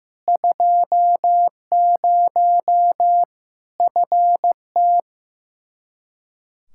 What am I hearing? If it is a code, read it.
Morse code, heard as a keyed tone, about 15 wpm, 700 Hz. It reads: 20 FT
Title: Practice QSO - Exchange 2 - Info - 15wpm